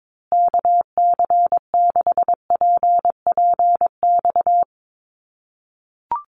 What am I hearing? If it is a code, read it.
Morse code, heard as a keyed tone, about 22 wpm, 700 Hz. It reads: KC6PPX
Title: USA Call Signs Repeated 3x 22wpm